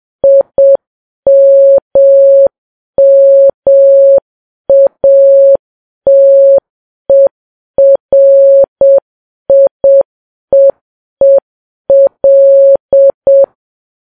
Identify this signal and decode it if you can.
Morse code, heard as a keyed tone, about 7 wpm, 550 Hz. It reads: IMMATERIEEL
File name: Files Web tls / Morse on9fv / morse7kps